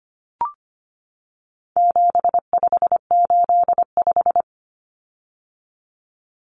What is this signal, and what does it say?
Morse code, heard as a keyed tone, about 25 wpm, 700 Hz. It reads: 7585
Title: Common Cut Numbers - 4-Digit 25wpm